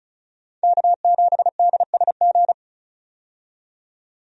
Morse code, heard as a keyed tone, about 36 wpm, 700 Hz. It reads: K7DSG
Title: USA Call Signs - 5 Characters - 35wpm